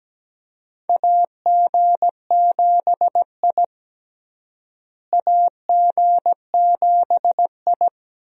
Morse code, heard as a keyed tone, about 17 wpm, 700 Hz. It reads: AG7I AG7I
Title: USA Call Signs - 4 Characters - Repeated 3x - 17wpm